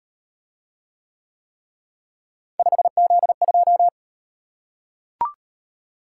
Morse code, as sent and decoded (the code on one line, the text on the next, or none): ..... --.. ..---
5Z2